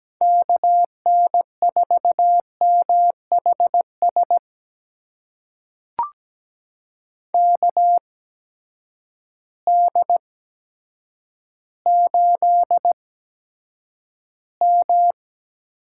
Morse code, heard as a keyed tone, about 17 wpm, spaced as for 8 wpm, 700 Hz. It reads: KN4MHS KD8M
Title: USA Call Signs - 6 Characters - 8x Character Spacing - 17wpm